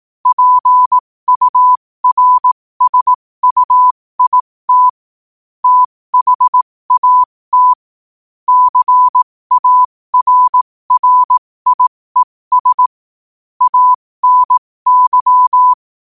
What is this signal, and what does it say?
Morse code, heard as a keyed tone, about 18 wpm, spaced as for 15 wpm, 1000 Hz. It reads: PURSUIT THAT CARRIES ANY
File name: qotdfiles / 15wpm